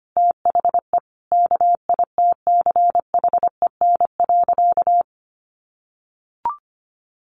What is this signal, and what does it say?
Morse code, heard as a keyed tone, about 25 wpm, 700 Hz. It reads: THE KITCHEN.
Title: Sentences from Top 1400 Words 25wpm